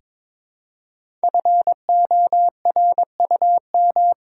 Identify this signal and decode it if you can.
Morse code, heard as a keyed tone, about 22 wpm, 700 Hz. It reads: FORUM
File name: 5 Letter Words - First 3 Letters Given - 22wpm